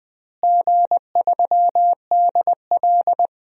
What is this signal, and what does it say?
Morse code, heard as a keyed tone, about 20 wpm, 700 Hz. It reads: G3DL
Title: DXCC-100 Call Signs - 4 Characters - Repeated 3x - 20wpm